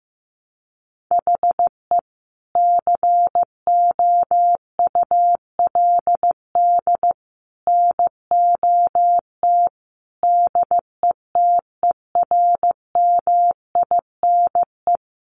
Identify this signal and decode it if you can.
Morse code, heard as a keyed tone, about 15 wpm, 700 Hz. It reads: HE COULD NOT DETERMINE
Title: Sentences from Top 700 Words 15wpm